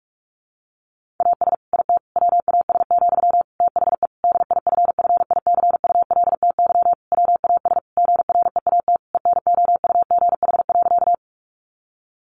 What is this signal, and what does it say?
Morse code, heard as a keyed tone, about 44 wpm, 700 Hz. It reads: AS IT WAS, THE DIFFICULTY WAS GREAT ENOUGH.
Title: Sentences from Top 1500 Words 45wpm